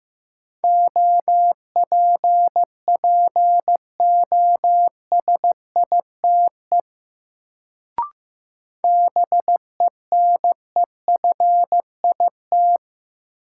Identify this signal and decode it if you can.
Morse code, heard as a keyed tone, about 15 wpm, 700 Hz. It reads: OPPOSITE BENEFIT
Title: Sets of 5 Words Encore - New Word Drill - 15wpm